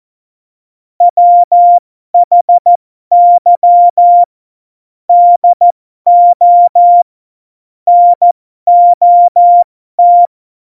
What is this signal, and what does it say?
Morse code, heard as a keyed tone, about 14 wpm, spaced as for 12 wpm, 700 Hz. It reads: WHY DO NOT
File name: Morsekurs / Witze